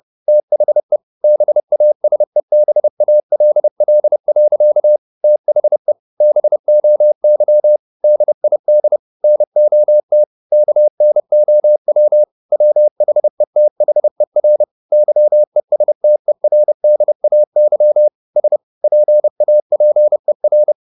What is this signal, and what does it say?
Morse code, heard as a keyed tone, about 30 wpm, 600 Hz. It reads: THE BASEBALL. THE BOY DID NOT KNOW WHETHER YESTERDAY S PAPER
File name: the old man and the sea / the old man and the sea 1 30wpm